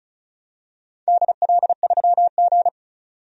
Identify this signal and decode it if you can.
Morse code, heard as a keyed tone, about 34 wpm, 700 Hz. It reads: DL3G
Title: DXCC-100 Call Signs - 4 Characters - 35wpm